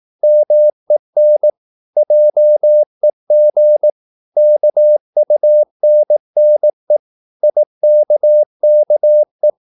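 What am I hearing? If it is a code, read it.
Morse code, heard as a keyed tone, about 18 wpm, 600 Hz. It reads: MEN JEG KUNNE IKKE
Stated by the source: Slow (QRS) Morse practice at 18 wpm, CW med Gnister.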